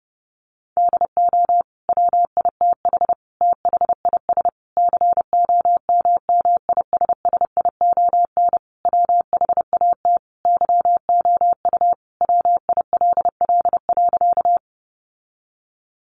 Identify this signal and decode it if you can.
Morse code, heard as a keyed tone, about 30 wpm, 700 Hz. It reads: DO WITH THIS COMMISSION WHAT YOU WILL.
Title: Sentences from Top 1400 Words 30wpm